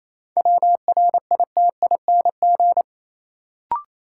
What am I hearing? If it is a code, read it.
Morse code, heard as a keyed tone, about 28 wpm, 700 Hz. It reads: WRITING
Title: Sentences from Top 800 Words - New Word Drill - 28wpm